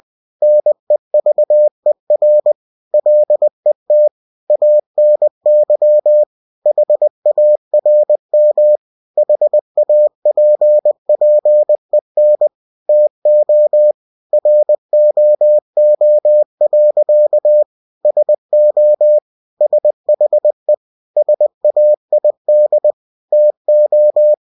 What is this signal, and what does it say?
Morse code, heard as a keyed tone, about 20 wpm, 600 Hz. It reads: NEVER LET ANY HARM HAPPEN TO ROO. SO SHE SAID TO
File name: cw-books / winnie-the-pooh / winnie-20wpm-page-0071